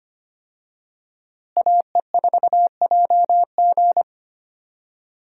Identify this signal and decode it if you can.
Morse code, heard as a keyed tone, about 25 wpm, 700 Hz. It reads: AE4JG
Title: USA Call Signs - 5 Characters - Repeated 3x - 25wpm